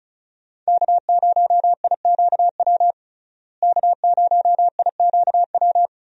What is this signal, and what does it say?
Morse code, heard as a keyed tone, about 36 wpm, 700 Hz. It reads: K0IQW K0IQW
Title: USA Call Signs - 5 Characters - Repeated 2x - 35wpm